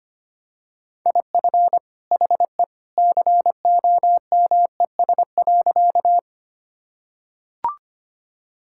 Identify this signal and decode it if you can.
Morse code, heard as a keyed tone, about 25 wpm, 690 Hz. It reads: IF HE COMES.
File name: practice / 25wpm / Sentences with 2-Word Phrases - 25wpm